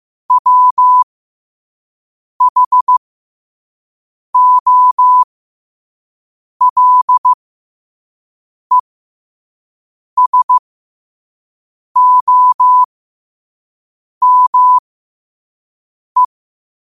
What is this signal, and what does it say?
Morse code, heard as a keyed tone, about 15 wpm, spaced as for 5 wpm, 1000 Hz. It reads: WHOLESOME
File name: qotdfiles / 5wpm